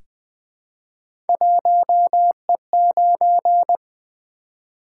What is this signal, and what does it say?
Morse code, heard as a keyed tone, about 20 wpm, 700 Hz. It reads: 1E9
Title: Three Letter-Number Combinations - 20wpm